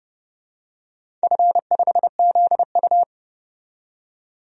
Morse code, heard as a keyed tone, about 30 wpm, 700 Hz. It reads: F5ZU
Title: DXCC-100 Call Signs - 4 Characters - 30wpm